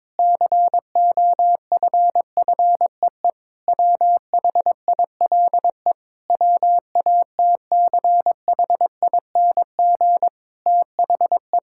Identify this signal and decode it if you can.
Morse code, heard as a keyed tone, about 22 wpm, 700 Hz. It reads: COFFEE WHILE WATCHING THE
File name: Difficult Sentences with Hints - 22wpm